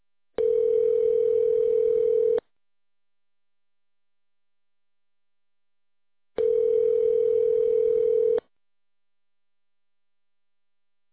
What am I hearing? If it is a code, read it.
Ringback tone (North America), 440 + 480 Hz.